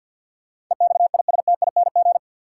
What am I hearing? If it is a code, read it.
Morse code, heard as a keyed tone, about 53 wpm, 700 Hz. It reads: EXISTING
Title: Sentences from Top 1500 Words - New Word Drill - 50wpm